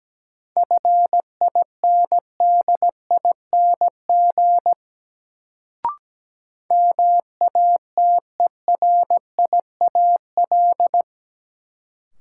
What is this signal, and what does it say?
Morse code, heard as a keyed tone, about 17 wpm, 700 Hz. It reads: FINDING MATERIAL
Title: Sentences from Top 1100 Words - New Word Drill Alt - 17wpm